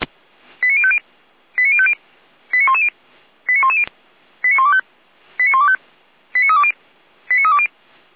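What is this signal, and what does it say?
Five-tone selective call (ZVEI), heard as quick runs of five tones, tones of about 70 ms. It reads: 89050 89050 89110 89110 89125 89125 89320 89320
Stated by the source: ZVEI five-tone call 89277 received by SDR, from the server of Feuerwehr Orlamünde (fire brigade).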